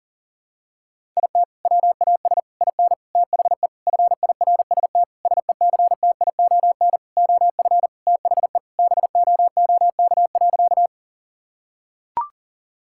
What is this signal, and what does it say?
Morse code, heard as a keyed tone, about 37 wpm, 700 Hz. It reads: IT WAS IN THE FIRST SECTION OF THE BOOK.
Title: Sentences from Top 1000 Words - Review Part 5 - 40wpm